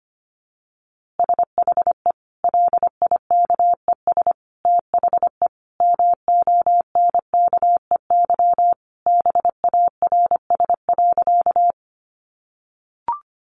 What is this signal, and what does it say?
Morse code, heard as a keyed tone, about 25 wpm, 700 Hz. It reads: SHE LIKES THE MONKEY BARS.
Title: Easy Phrases 25wpm